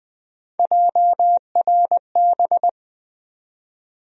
Morse code, heard as a keyed tone, about 20 wpm, 700 Hz. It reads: JRB
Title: USA Call Sign Postfixes - 3 Letters 20wpm